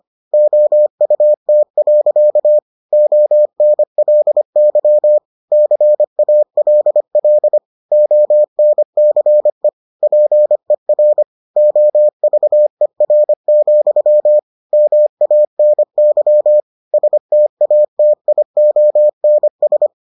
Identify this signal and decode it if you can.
Morse code, heard as a keyed tone, about 25 wpm, 600 Hz. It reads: OUT. ONLY CALL ONCE PER OVER, MANY STATIONS